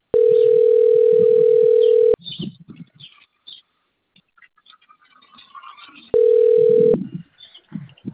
Ringback tone (North America), 440 + 480 Hz.
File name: out- - - - - .18752